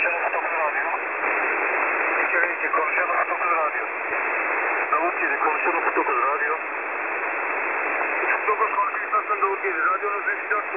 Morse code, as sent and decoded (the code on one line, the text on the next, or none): none